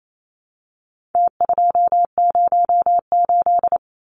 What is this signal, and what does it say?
Morse code, heard as a keyed tone, about 28 wpm, 700 Hz. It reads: T208
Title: Numbers - Contest Style - 4-Digit 28wpm